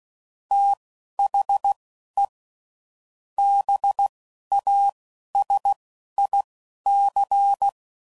Morse code, heard as a keyed tone, about 16 wpm, spaced as for 11 wpm, 800 Hz. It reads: THE BASIC
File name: CW Code Groups @10WPM / Technical Subjects / AND Gate